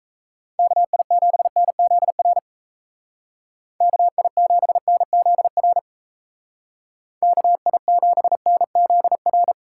Morse code, heard as a keyed tone, about 34 wpm, 700 Hz. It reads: KI7NZR KI7NZR KI7NZR
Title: Speed Racing - USA 2x3 Call Signs 28wpm